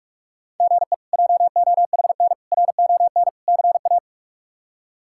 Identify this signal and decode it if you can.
Morse code, heard as a keyed tone, about 45 wpm, 700 Hz. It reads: GE JOHN RON CA
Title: K1USN SST QSO Practice - 45wpm